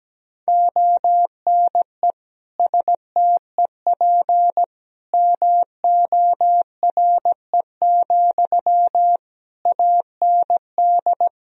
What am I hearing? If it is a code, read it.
Morse code, heard as a keyed tone, about 17 wpm, 700 Hz. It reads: ONE STEP MORE, AND
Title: Sentences from Top 1000 Words - Review Part 3 - 17wpm